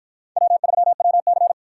Morse code, heard as a keyed tone, about 50 wpm, 700 Hz. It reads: W3WC